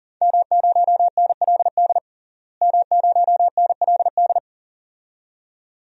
Morse code, heard as a keyed tone, about 40 wpm, 700 Hz. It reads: M0NLD M0NLD
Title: DXCC-100 Call Signs - 5 Characters - Repeated 2x - 40wpm